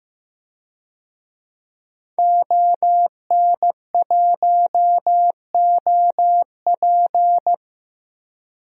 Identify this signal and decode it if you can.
Morse code, heard as a keyed tone, about 15 wpm, 700 Hz. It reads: ON1OP
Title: DXCC-100 Call Signs - 5 Characters - Repeated 2x - 15wpm